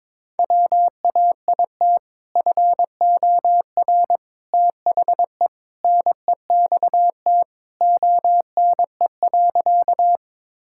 Morse code, heard as a keyed tone, about 22 wpm, 700 Hz. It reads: WAIT FOR THE NEXT ONE.
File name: practice / 22wpm / Sentences with 2-Word Phrases - 22wpm